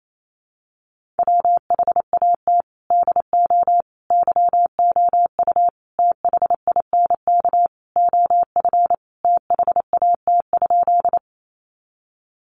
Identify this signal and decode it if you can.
Morse code, heard as a keyed tone, about 28 wpm, 700 Hz. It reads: WHAT DO YOU THINK OF THAT?
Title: Sentences from Top 200 Words 28wpm